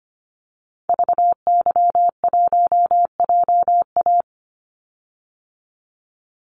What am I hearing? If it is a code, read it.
Morse code, heard as a keyed tone, about 25 wpm, 700 Hz. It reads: VY1JA
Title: Call Signs Repeated 3x - 25wpm